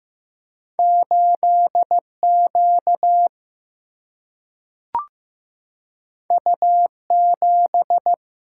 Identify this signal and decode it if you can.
Morse code, heard as a keyed tone, about 15 wpm, 700 Hz. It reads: 8 Q U 7
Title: Single-Letter Single-Number 15wpm